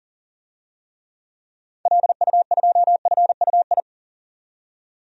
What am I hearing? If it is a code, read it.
Morse code, heard as a keyed tone, about 37 wpm, 700 Hz. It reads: LU2FUI
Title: DXCC-100 Call Signs - 6 Characters - 40wpm